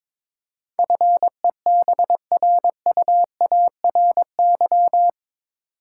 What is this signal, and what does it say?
Morse code, heard as a keyed tone, about 22 wpm, 700 Hz. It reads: FEBRUARY
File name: practice / 22wpm / Eight Letter Words - 22wpm